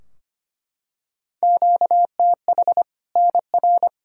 Morse code, heard as a keyed tone, about 25 wpm, 700 Hz. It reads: QTH NR
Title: Practice QSO - Exchange 1 - Beginning - 25wpm